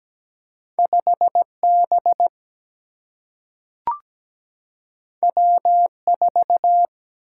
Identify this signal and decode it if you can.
Morse code, heard as a keyed tone, about 17 wpm, 700 Hz. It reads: 5B W4
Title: Single-Letter Single-Number 17wpm